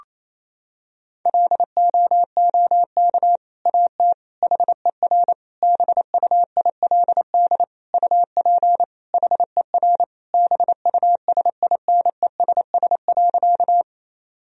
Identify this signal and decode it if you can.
Morse code, heard as a keyed tone, about 28 wpm, 700 Hz. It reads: LOOK AT HER BUILD UP HER BUSINESS.
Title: Sentences from Top 1000 Words - Review Part 1 - 28wpm